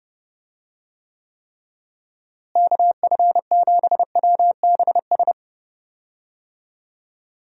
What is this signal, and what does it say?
Morse code, heard as a keyed tone, about 30 wpm, 700 Hz. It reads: KF7WBS
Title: USA Call Signs - 6 Characters - Repeated 3x - 30wpm